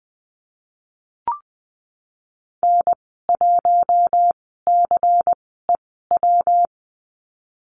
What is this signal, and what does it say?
Morse code, heard as a keyed tone, about 20 wpm, spaced as for 15 wpm, 700 Hz. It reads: N1CEW
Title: 213 Lesson 34g - Number 8 - Call Signs 20wpm